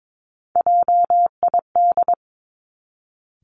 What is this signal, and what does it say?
Morse code, heard as a keyed tone, about 22 wpm, 700 Hz. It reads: JID
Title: Three Challenging Letter-Number Combinations 22wpm